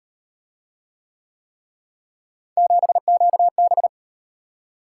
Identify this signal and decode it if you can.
Morse code, heard as a keyed tone, about 38 wpm, 700 Hz. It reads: 7QB